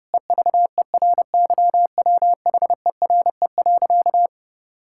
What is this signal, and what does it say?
Morse code, heard as a keyed tone, about 32 wpm, 700 Hz. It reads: EVERYWHERE.
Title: Sentences from Top 1600 Words 30wpm